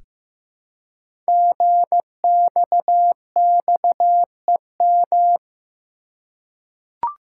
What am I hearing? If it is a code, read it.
Morse code, heard as a keyed tone, about 15 wpm, 700 Hz. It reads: GXXEM
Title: Code Groups - 5 Letters - 15wpm